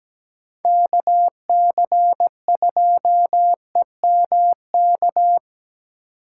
Morse code, heard as a keyed tone, about 17 wpm, 700 Hz. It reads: K C 2 E M K
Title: USA Call Signs - 6 Characters - 6x Character Spacing - 17wpm